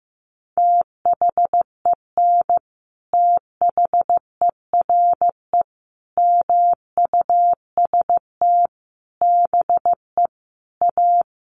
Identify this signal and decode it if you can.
Morse code, heard as a keyed tone, about 15 wpm, 700 Hz. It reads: THEN THERE MUST BE A
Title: Sentences from Top 1400 Words 15wpm